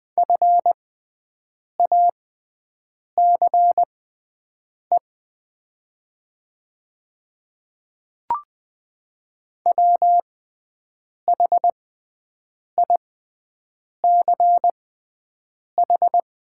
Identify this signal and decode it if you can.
Morse code, heard as a keyed tone, about 20 wpm, spaced as for 7 wpm, 700 Hz. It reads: FACE WHICH
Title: 6x Character Spacing - Top 300 Words - 20wpm